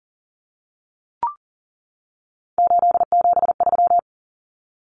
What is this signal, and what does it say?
Morse code, heard as a keyed tone, about 40 wpm, 700 Hz. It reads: 873
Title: Common Cut Numbers - 3-Digit 40wpm